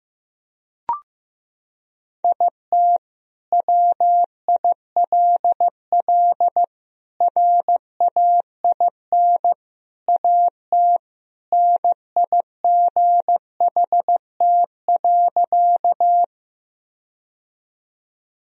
Morse code, heard as a keyed tone, about 15 wpm, 700 Hz. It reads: IT WILL RAIN AT NIGHT.
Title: Sentences from Top 1000 Words - Review Part 1 - 15wpm